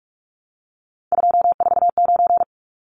Morse code, heard as a keyed tone, about 44 wpm, 700 Hz. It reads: 249
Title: Numbers - 3-Digit 45wpm